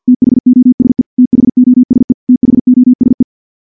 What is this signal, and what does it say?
Morse code, heard as a keyed tone, about 46 wpm, 250 Hz. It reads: THOSE THOSE THOSE
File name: morse / morse-training-50WPM